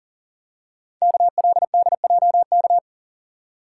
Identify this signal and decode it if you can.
Morse code, heard as a keyed tone, about 42 wpm, 700 Hz. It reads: KLDJK
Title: Code Groups - 5 Letters - 40wpm